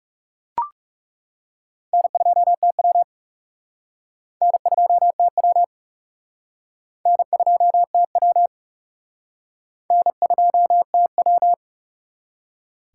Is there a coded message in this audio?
Yes: Morse code from 1.9 to 11.5 s, heard as a keyed tone, about 39 wpm, 700 Hz. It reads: N2TW N2TW N2TW N2TW